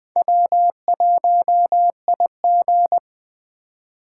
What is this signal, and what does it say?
Morse code, heard as a keyed tone, about 21 wpm, 700 Hz. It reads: W1IG